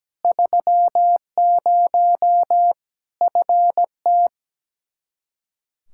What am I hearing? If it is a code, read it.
Morse code, heard as a keyed tone, about 17 wpm, 700 Hz. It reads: 30 FT
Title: Practice QSO - Exchange 2 - Info - 17wpm